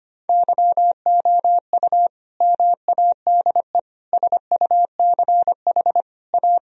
Morse code, heard as a keyed tone, about 25 wpm, 700 Hz. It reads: YOU MADE SUCH A
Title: Sentences from Top 1000 Words - Review Part 4 - 25wpm